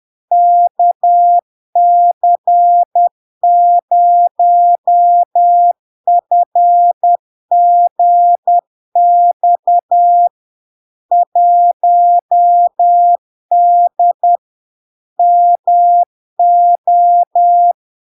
Morse code, heard as a keyed tone, about 10 wpm, 700 Hz. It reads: KC0FGX 1D MO
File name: LARK-FD-2022-Log-in-CW-10wpm-700-Hz